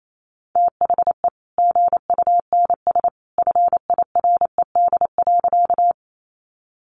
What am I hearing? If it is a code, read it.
Morse code, heard as a keyed tone, about 28 wpm, 700 Hz. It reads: THE GUNS FIRED.
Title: A Sound Of Thunder 28wpm